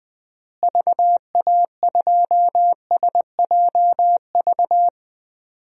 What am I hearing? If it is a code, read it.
Morse code, heard as a keyed tone, about 20 wpm, 700 Hz. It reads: VA2SJV
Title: DXCC-100 Call Signs - 6 Characters - Repeated 2x - 20wpm